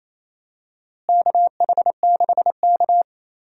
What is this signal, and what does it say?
Morse code, heard as a keyed tone, about 27 wpm, 700 Hz. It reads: KH6K
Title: DXCC-100 Call Signs - 4 Characters - Repeated 3x - 28wpm